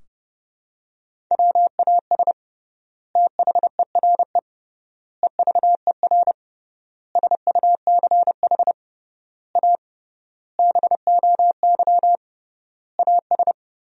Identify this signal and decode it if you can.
Morse code, heard as a keyed tone, about 28 wpm, 700 Hz. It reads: WAS THERE EVER SUCH A BOY AS
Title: 3x Word-Spacing - Sentences from Top 400 Words - 30wpm